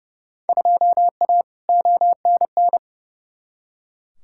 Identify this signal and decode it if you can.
Morse code, heard as a keyed tone, about 30 wpm, 700 Hz. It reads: 2A ONN
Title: Field Day Exchanges Encore - 30wpm